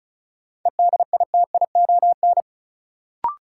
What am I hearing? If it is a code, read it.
Morse code, heard as a keyed tone, about 36 wpm, 700 Hz. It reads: EDITION
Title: Sets of 5 Words - New Word Drill - 35wpm